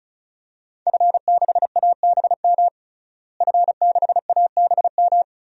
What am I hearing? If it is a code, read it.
Morse code, heard as a keyed tone, about 36 wpm, 700 Hz. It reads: F6ABM F6ABM
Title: DXCC-100 Call Signs - 5 Characters - Repeated 3x - 35wpm